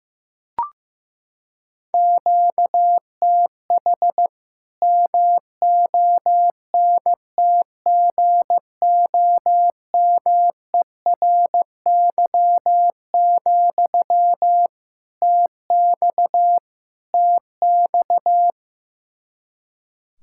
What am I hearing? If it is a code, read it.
Morse code, heard as a keyed tone, about 15 wpm, 700 Hz. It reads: QTH MONTGOMERY, TX TX
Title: K1USN SST QSO Practice 15wpm